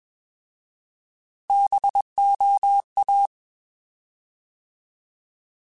Morse code, heard as a keyed tone, about 21 wpm, 800 Hz. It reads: BOA
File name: Alphabetical / Alphabetical mp3s / Bs @20